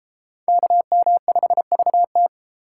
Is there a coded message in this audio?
Yes: Morse code from 0.5 to 2.3 s, heard as a keyed tone, about 31 wpm, 700 Hz. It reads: KM5VT